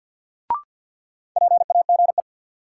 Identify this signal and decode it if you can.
Morse code, heard as a keyed tone, about 47 wpm, 700 Hz. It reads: PAGE
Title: Sentences from Top 1000 Words - Review Part 1 - New Word Drill - 50wpm